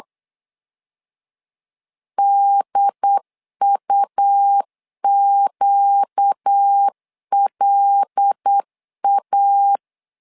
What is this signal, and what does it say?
Morse code, heard as a keyed tone, about 8 wpm, 800 Hz. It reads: DUQLA
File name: LEZIONI / 5 LBVC / 5 L 8wpm 2831